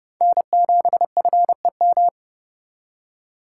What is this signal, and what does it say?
Morse code, heard as a keyed tone, about 30 wpm, 700 Hz. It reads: N7FEM
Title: USA Call Signs - 5 Characters - 4x Character Spacing - 30wpm